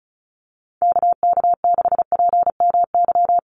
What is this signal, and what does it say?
Morse code, heard as a keyed tone, about 34 wpm, 700 Hz. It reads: KK6PMY
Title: USA Call Signs 35wpm